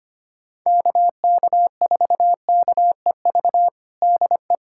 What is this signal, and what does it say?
Morse code, heard as a keyed tone, about 25 wpm, 700 Hz. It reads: KK4KEV DE
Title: USA Callsign DE Callsign - 6 Characters - 25wpm